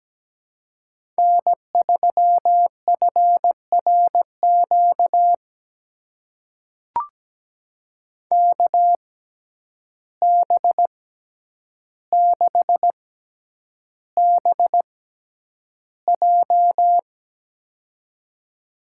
Morse code, heard as a keyed tone, about 17 wpm, spaced as for 9 wpm, 700 Hz. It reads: N3FRQ KB6BJ
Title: USA Call Signs - 6x Character Spacing - 17wpm